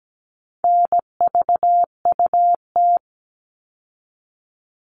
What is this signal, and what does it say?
Morse code, heard as a keyed tone, about 17 wpm, 700 Hz. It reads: NVUT
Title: All Cut Numbers - 4-Digit 17wpm